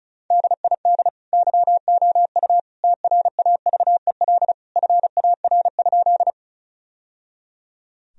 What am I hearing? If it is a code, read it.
Morse code, heard as a keyed tone, about 34 wpm, 700 Hz. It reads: DID YOU TRAVEL FAR?
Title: Sentences from Top 1000 Words - Review Part 2 - 35wpm